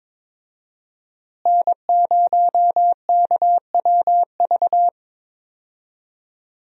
Morse code, heard as a keyed tone, about 22 wpm, 700 Hz. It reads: N0KWV
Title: USA Call Signs - 5 Characters - Repeated 3x - 22wpm